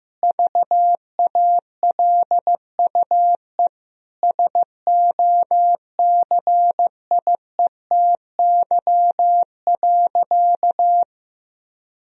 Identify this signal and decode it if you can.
Morse code, heard as a keyed tone, about 15 wpm, 700 Hz. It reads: VALUE SOCIETY.
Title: Sentences from Top 1000 Words - Review Part 4 - 15wpm